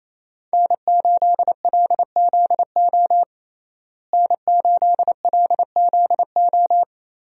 Morse code, heard as a keyed tone, about 28 wpm, 700 Hz. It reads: N8LZO N8LZO
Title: USA Call Signs - 5 Characters - Repeated 3x - 28wpm